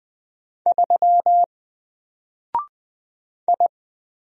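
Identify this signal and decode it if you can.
Morse code, heard as a keyed tone, about 20 wpm, 700 Hz. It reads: 3 I
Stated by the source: Single letter and number Morse drill at 20 wpm, from N4LCD.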